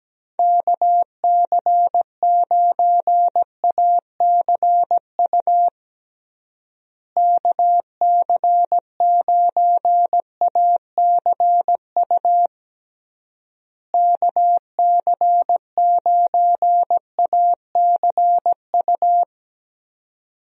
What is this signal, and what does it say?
Morse code, heard as a keyed tone, about 17 wpm, 700 Hz. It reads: KC9ACU KC9ACU KC9ACU
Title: USA Call Signs - 6 Characters - Repeated 3x - 17wpm